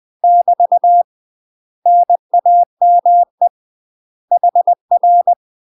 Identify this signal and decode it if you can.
Morse code, heard as a keyed tone, about 20 wpm, 700 Hz. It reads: <BT> NAME HR
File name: UK MASTER QSOs TRAINING / 20WPM / UK MASTER QSOs 0052